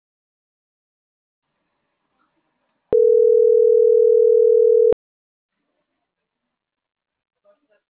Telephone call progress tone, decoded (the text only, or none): ringback tone